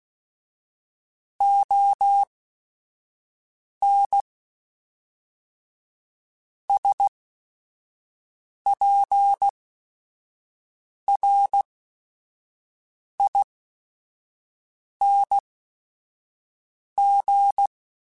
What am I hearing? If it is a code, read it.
Morse code, heard as a keyed tone, about 16 wpm, spaced as for 5 wpm, 800 Hz. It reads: ON SPRING
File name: QSOs / QSOs-mp3s / WA7QOX 1@5